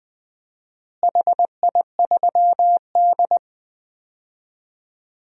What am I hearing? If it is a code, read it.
Morse code, heard as a keyed tone, about 20 wpm, 700 Hz. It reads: HI3D
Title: DXCC-100 Call Signs - 4 Characters - Repeated 3x - 20wpm